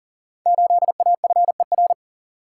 Morse code, heard as a keyed tone, about 39 wpm, 700 Hz. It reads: 8AFER
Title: Code Groups - 5 Letter:Numbers - 40wpm